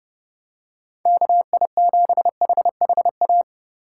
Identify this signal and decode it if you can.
Morse code, heard as a keyed tone, about 31 wpm, 700 Hz. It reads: KI7HHA
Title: USA Call Signs - 6 Characters - 4x Character Spacing - 30wpm